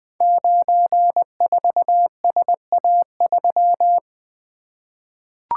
Morse code, heard as a keyed tone, about 20 wpm, 700 Hz. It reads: 94SA3
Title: Code Groups - 5 Letter:Numbers - 20wpm